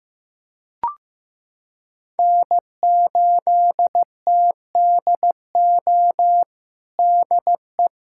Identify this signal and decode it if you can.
Morse code, heard as a keyed tone, about 15 wpm, 700 Hz. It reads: N8TDO DE
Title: USA Callsign DE Callsign - 5 Characters - 15wpm